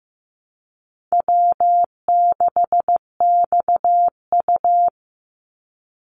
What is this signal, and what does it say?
Morse code, heard as a keyed tone, about 15 wpm, 700 Hz. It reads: W6XU
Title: Call Signs Repeated 3x - 15wpm